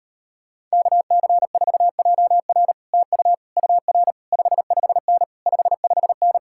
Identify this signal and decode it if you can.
Morse code, heard as a keyed tone, about 38 wpm, 700 Hz. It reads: KC4JR TU UR 55N 55N